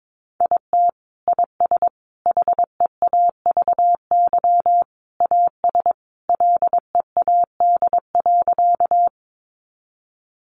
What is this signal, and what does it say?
Morse code, heard as a keyed tone, about 22 wpm, 700 Hz. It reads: IT IS HEAVY AS LEAD.
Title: Sentences from Top 1100 Words 22wpm